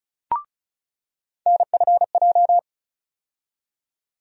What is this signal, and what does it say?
Morse code, heard as a keyed tone, about 36 wpm, 700 Hz. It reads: NFJ
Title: USA Call Sign Postfixes - 3 Letters 35wpm